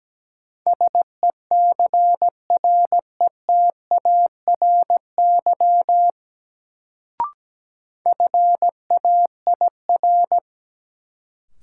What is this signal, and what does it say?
Morse code, heard as a keyed tone, about 17 wpm, 700 Hz. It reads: SECRETARY FAIR
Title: Sentences from Top 1300 Words - New Word Drill Alt - 17wpm